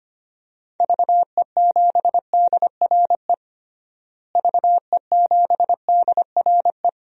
Morse code, heard as a keyed tone, about 25 wpm, 700 Hz. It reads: VE7DRE VE7DRE
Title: DXCC-100 Call Signs - 6 Characters - Repeated 2x - 25wpm